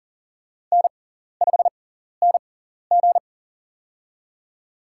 Morse code, heard as a keyed tone, about 40 wpm, spaced as for 14 wpm, 700 Hz. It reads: N5NG
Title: USA Call Signs - 4 Characters - 6x Character Spacing - 40wpm